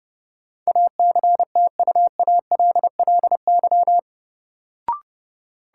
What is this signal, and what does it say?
Morse code, heard as a keyed tone, about 30 wpm, 700 Hz. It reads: ACTUALLY
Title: Sentences from Top 700 Words - New Word Drill - 30wpm